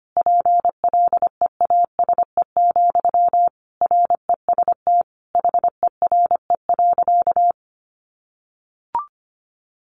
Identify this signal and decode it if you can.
Morse code, heard as a keyed tone, about 25 wpm, 700 Hz. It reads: PLEASE, REST HERE.
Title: Sentences from Top 1500 Words 25wpm